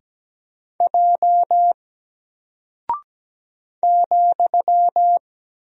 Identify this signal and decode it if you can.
Morse code, heard as a keyed tone, about 17 wpm, 700 Hz. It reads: J ,
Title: Single Character (Letter:Number:Punctuation:Prosign) 17wpm